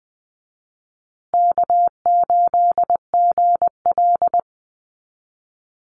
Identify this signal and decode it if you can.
Morse code, heard as a keyed tone, about 20 wpm, 700 Hz. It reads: K8GL
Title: Easy Call Signs Repeated 2x - 20wpm